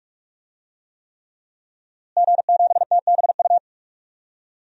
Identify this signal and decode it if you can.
Morse code, heard as a keyed tone, about 41 wpm, 700 Hz. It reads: G7TBU
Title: DXCC-100 Call Signs - 5 Characters - Repeated 3x - 45wpm